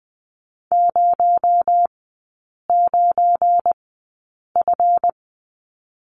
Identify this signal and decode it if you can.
Morse code, heard as a keyed tone, about 20 wpm, spaced as for 8 wpm, 700 Hz. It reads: 09F
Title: 218 Lesson 35e - Number 0 - Three Characters 20wpm